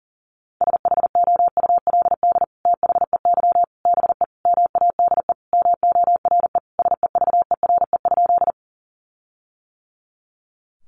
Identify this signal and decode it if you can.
Morse code, heard as a keyed tone, about 40 wpm, 700 Hz. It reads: SHOULD THEY BE MADE MORE SEVERE?
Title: Sentences from Top 1400 Words 40wpm